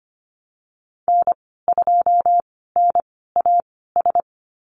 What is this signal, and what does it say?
Morse code, heard as a keyed tone, about 25 wpm, spaced as for 16 wpm, 700 Hz. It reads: N2NAS
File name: Lesson 15g - Number 5 - Call Signs 25wpm